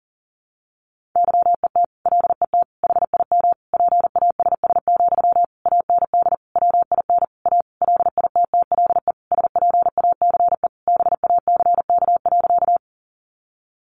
Morse code, heard as a keyed tone, about 40 wpm, 700 Hz. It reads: YET LET HIM PASS, AND WIN A LITTLE SPACE BACK.
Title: Sentences from Top 1500 Words 40wpm